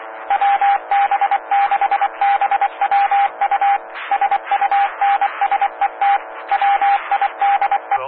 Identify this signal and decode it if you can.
Morse code, heard as a keyed tone, about 24 wpm, 800 Hz. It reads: WB6BWU SUNSET WIDE
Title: Claremont, CA Police, Amateur Radio